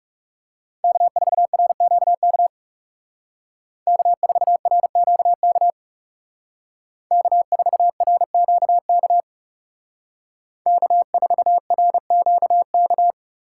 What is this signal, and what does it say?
Morse code, heard as a keyed tone, about 37 wpm, 700 Hz. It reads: K4RQK K4RQK K4RQK K4RQK
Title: Speed Racing - USA 1x3 Call Signs 30wpm